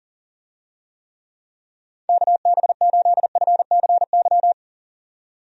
Morse code, heard as a keyed tone, about 38 wpm, 700 Hz. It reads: KB8FCY
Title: USA Call Signs - 6 Characters - 40wpm